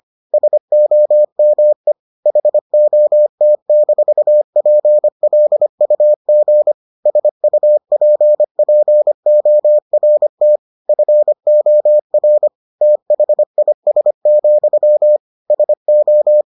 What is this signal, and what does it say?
Morse code, heard as a keyed tone, about 25 wpm, 600 Hz. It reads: SOME HOT-PLUG SUPPORT FOR THIS, SO